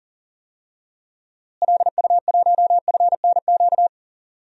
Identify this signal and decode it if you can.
Morse code, heard as a keyed tone, about 40 wpm, 700 Hz. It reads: LU1FNQ
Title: DXCC-100 Call Signs - 6 Characters - Repeated 2x - 40wpm